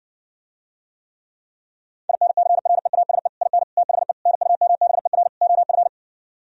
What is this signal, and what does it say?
Morse code, heard as a keyed tone, about 60 wpm, 700 Hz. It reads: INCREASE IN THE NUMBER OF